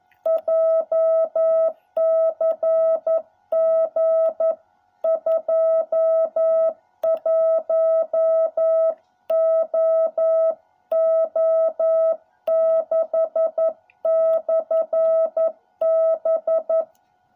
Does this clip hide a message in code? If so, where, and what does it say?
Yes: Morse code from 0.3 to 16.8 s, heard as a keyed tone, about 11 wpm, 660 Hz. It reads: JCG 21OO6/B